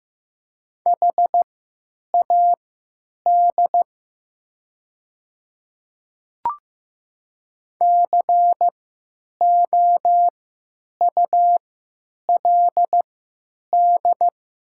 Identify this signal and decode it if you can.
Morse code, heard as a keyed tone, about 15 wpm, spaced as for 9 wpm, 700 Hz. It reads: HAD COULD
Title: 3x Character Spacing - Top 500 Words - 15wpm